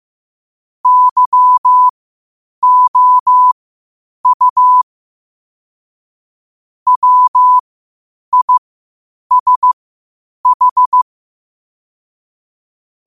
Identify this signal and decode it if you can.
Morse code, heard as a keyed tone, about 15 wpm, spaced as for 8 wpm, 1000 Hz. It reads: YOU WISH
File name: qotdfiles / 8wpm